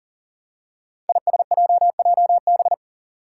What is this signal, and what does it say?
Morse code, heard as a keyed tone, about 38 wpm, 700 Hz. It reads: ISJJB
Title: Five Challenging Letter-Number Combinations 40wpm